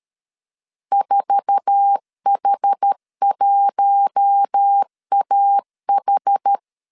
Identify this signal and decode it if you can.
Morse code, heard as a keyed tone, about 13 wpm, 800 Hz. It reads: 4H1AH
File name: CW / DET / 1 9 1500C S C 12wpm 3811